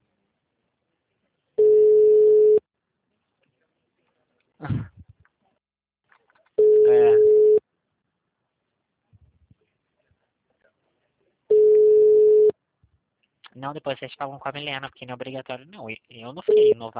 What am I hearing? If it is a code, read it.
Ringback tone (Europe, 425 Hz), with speech.